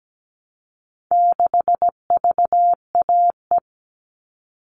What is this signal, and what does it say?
Morse code, heard as a keyed tone, about 17 wpm, 700 Hz. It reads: 6VAE
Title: All Cut Numbers - 4-Digit 17wpm